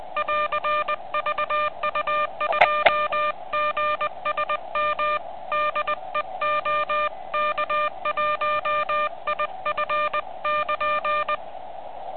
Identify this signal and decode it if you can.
Morse code, heard as a keyed tone, about 20 wpm, 1200 Hz. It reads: <AR> VU2GSM DE OK1IF <KN>
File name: ARCHIV / QCX / vu2gsm2